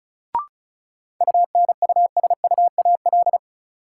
Morse code, heard as a keyed tone, about 36 wpm, 700 Hz. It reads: UNUSUAL
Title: Sentences from Top 1400 Words - New Word Drill - 35wpm